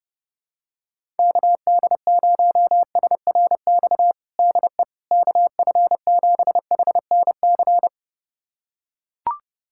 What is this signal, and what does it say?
Morse code, heard as a keyed tone, about 29 wpm, 700 Hz. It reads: KD0SRX DE KF7HNC
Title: USA Callsign DE Callsign - 6 Characters - 30wpm